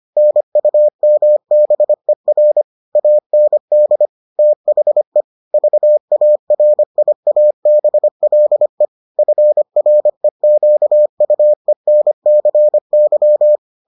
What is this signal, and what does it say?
Morse code, heard as a keyed tone, about 25 wpm, 600 Hz. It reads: NUMBER AND THE VARIABLE FREQUENCY